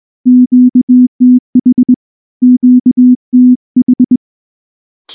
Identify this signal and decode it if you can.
Morse code, heard as a keyed tone, about 20 wpm, 250 Hz. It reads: QTH QTH
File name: morse / morse-training-20WPM